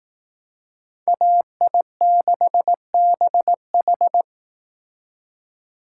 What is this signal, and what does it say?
Morse code, heard as a keyed tone, about 18 wpm, 700 Hz. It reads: AI6BH
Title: Speed Racing - USA 2x2 Call Signs 15wpm